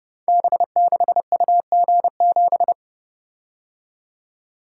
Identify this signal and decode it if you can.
Morse code, heard as a keyed tone, about 29 wpm, 700 Hz. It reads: B6UG7
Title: Code Groups - 5 Letter:Numbers - 30wpm